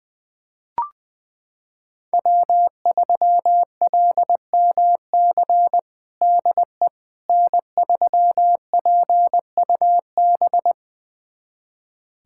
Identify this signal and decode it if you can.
Morse code, heard as a keyed tone, about 20 wpm, 700 Hz. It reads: W3LMC DE N3PUB
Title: USA Callsign DE Callsign - 5 Characters - 20wpm